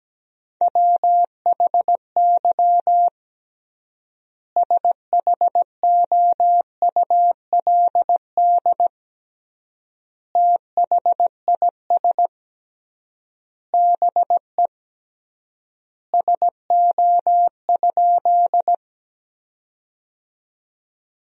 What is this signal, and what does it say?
Morse code, heard as a keyed tone, about 17 wpm, 700 Hz. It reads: WHY SHOULD THIS BE SO?
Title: 3x Word-Spacing - Sentences from Top 300 Words - 17wpm